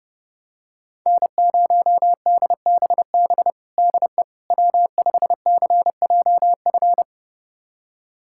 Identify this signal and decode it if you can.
Morse code, heard as a keyed tone, about 30 wpm, 700 Hz. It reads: N0DBB DE W5CJF
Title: USA Callsign DE Callsign - 5 Characters - 30wpm